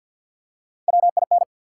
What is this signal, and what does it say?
Morse code, heard as a keyed tone, about 50 wpm, 700 Hz. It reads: WIN
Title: Three Letter Words - 50wpm